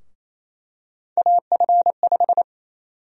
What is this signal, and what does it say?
Morse code, heard as a keyed tone, about 28 wpm, 700 Hz. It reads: AF5
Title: USA Call Sign Prefixes - 28wpm